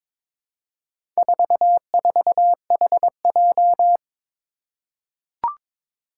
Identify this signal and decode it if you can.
Morse code, heard as a keyed tone, about 23 wpm, 700 Hz. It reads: 44HJ
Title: Four Challenging Letter-Number Combinations 22wpm